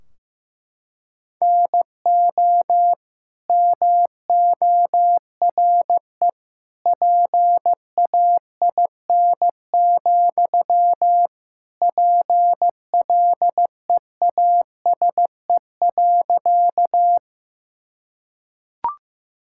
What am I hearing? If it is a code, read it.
Morse code, heard as a keyed tone, about 15 wpm, 700 Hz. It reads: NO MORE PAIN, PLEASE.
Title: Sentences from Top 1600 Words 15wpm